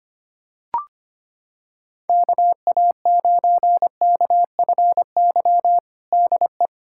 Morse code, heard as a keyed tone, about 25 wpm, 700 Hz. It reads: KA9KFY DE
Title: USA Callsign DE Callsign - 6 Characters - 25wpm